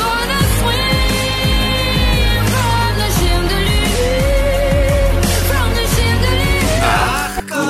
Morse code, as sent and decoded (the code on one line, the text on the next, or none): none